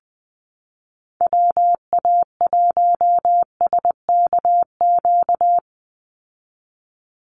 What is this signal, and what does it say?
Morse code, heard as a keyed tone, about 20 wpm, 700 Hz. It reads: WA1SKQ
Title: Call Signs Repeated 2x - 20wpm